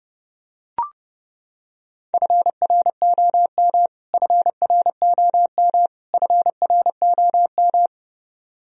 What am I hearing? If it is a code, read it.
Morse code, heard as a keyed tone, about 31 wpm, 700 Hz. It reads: FROM FROM FROM
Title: Top 100 Words Repeated 30wpm